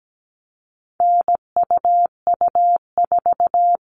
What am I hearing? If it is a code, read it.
Morse code, heard as a keyed tone, about 17 wpm, 700 Hz. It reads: NUU4
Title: All Cut Numbers - 4-Digit 17wpm